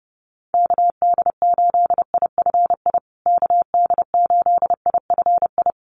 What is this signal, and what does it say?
Morse code, heard as a keyed tone, about 30 wpm, 700 Hz. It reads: KD8IFI KD8IFI
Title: USA Call Signs Repeated 2x 30wpm